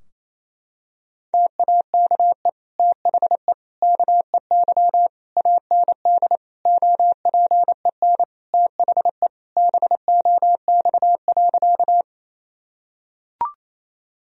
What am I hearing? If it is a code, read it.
Morse code, heard as a keyed tone, about 28 wpm, 700 Hz. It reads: TAKE THE KEY AND OPEN THE BOX.
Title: Sentences from Top 1000 Words - Review Part 4 - 28wpm